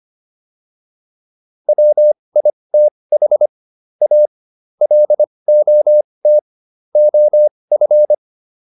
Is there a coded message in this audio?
Yes: Morse code from 1.7 to 8.1 s, heard as a keyed tone, about 25 wpm, spaced as for 20 wpm, 600 Hz. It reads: WITH A LOT OF